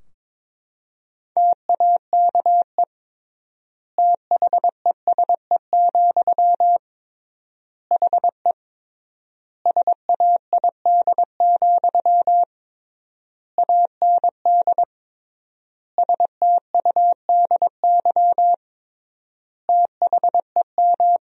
Morse code, heard as a keyed tone, about 22 wpm, 700 Hz. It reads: TAKE THESE, HE SAID, AND STUDY THEM
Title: 3x Word-Spacing - Sentences from Top 300 Words - 22wpm